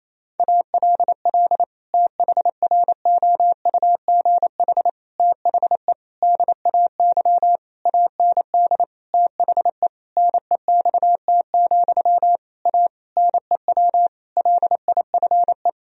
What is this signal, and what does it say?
Morse code, heard as a keyed tone, about 28 wpm, 700 Hz. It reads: ALL THROUGH THE DAY AND THE NEXT, A NEW LIFE
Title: Sentences from Top 1000 Words - Review Part 1 - 28wpm